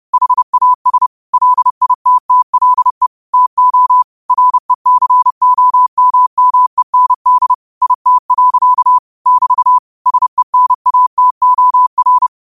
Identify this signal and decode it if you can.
Morse code, heard as a keyed tone, about 30 wpm, 1000 Hz. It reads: HAS LITTLE TO RECOMMEND IT. <BT> SENATOR